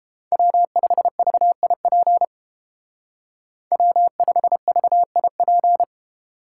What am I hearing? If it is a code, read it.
Morse code, heard as a keyed tone, about 30 wpm, 700 Hz. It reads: W5VIP W5VIP